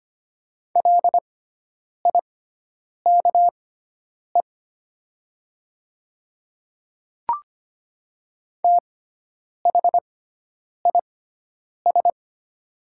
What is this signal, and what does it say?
Morse code, heard as a keyed tone, about 25 wpm, spaced as for 9 wpm, 700 Hz. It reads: LIKE THIS
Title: 6x Character Spacing - Top 100 Words - 25wpm